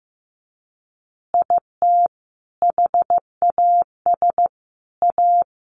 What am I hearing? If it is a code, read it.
Morse code, heard as a keyed tone, about 15 wpm, 700 Hz. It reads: IT HAS A
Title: Easy Phrases 15wpm